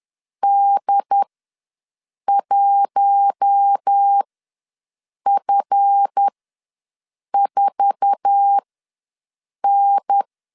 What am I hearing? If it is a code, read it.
Morse code, heard as a keyed tone, about 10 wpm, spaced as for 6 wpm, 800 Hz. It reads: D1F4N
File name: LEZIONI / 9 2867 ws 10wpm 4215